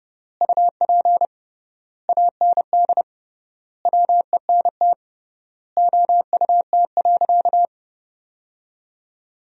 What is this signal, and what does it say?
Morse code, heard as a keyed tone, about 32 wpm, 700 Hz. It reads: UP AND WENT OUT.
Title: 3x Word-Spacing - Sentences from Top 300 Words - 30wpm